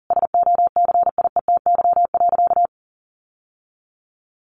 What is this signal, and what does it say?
Morse code, heard as a keyed tone, about 40 wpm, 700 Hz. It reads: SOCIETY.
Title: Sentences from Top 1500 Words 40wpm